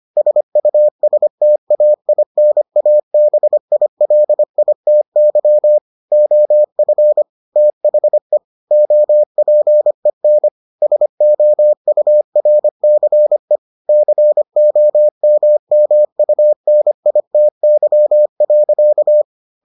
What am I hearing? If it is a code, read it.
Morse code, heard as a keyed tone, about 25 wpm, 600 Hz. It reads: SUSTAINABILITY OF THE OPEN SOURCE COMMUNITY.